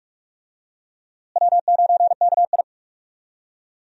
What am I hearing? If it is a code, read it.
Morse code, heard as a keyed tone, about 46 wpm, 700 Hz. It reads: W9KI
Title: USA Call Signs - 4 Characters - Repeated 3x - 45wpm